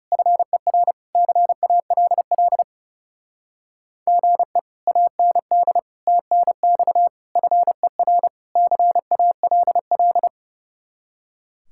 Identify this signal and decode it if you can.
Morse code, heard as a keyed tone, about 37 wpm, 700 Hz. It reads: FER CALL GE AND TNX FER CALL
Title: Speed-Racing Common QSO Phrases - 30wpm